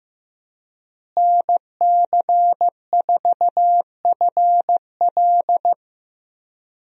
Morse code, heard as a keyed tone, about 15 wpm, 700 Hz. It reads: NC4FL
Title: USA Call Signs - 5 Characters - Repeated 2x - 15wpm